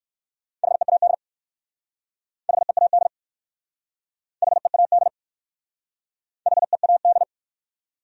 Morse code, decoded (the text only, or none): HEAD HEAD HEAD HEAD